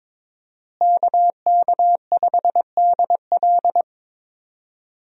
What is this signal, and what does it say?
Morse code, heard as a keyed tone, about 22 wpm, 700 Hz. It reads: KK5DL
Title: USA Call Signs - 5 Characters - Repeated 3x - 22wpm